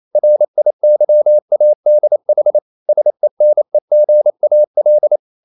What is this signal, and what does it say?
Morse code, heard as a keyed tone, about 28 wpm, 600 Hz. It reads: RIYADH SENEGAL
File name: CW / Practice / NationalCapitals28